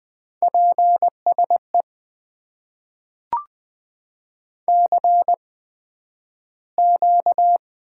Morse code, heard as a keyed tone, about 20 wpm, spaced as for 11 wpm, 700 Hz. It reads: PSE CQ
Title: Top 90 QSO Elements - 8x Character Spacing - 20wpm